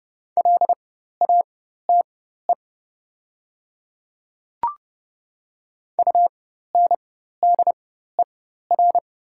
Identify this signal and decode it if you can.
Morse code, heard as a keyed tone, about 31 wpm, spaced as for 14 wpm, 690 Hz. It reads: LATE UNDER
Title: 4x Character Spacing - Top 300 Words - 30wpm